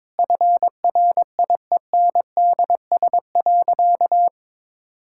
Morse code, heard as a keyed tone, about 22 wpm, 700 Hz. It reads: FRIENDS.
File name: practice / 22wpm / Sentences with 3-Word Phrases - 22wpm